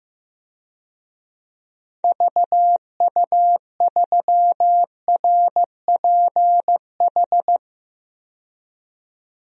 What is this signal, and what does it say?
Morse code, heard as a keyed tone, about 15 wpm, 700 Hz. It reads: VU3RPH
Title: DXCC-100 Call Signs - 6 Characters - 15wpm